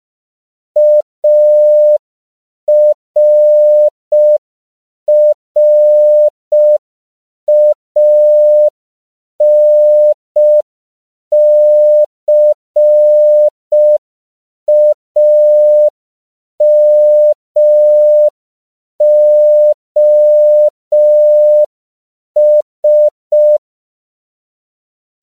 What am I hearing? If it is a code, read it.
Morse code, heard as a keyed tone, about 5 wpm, 600 Hz. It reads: ARRANCAMOS
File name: 5PPM